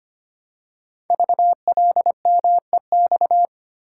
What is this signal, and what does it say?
Morse code, heard as a keyed tone, about 25 wpm, 700 Hz. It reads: VLMEX
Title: Code Groups - 5 Letters - 25wpm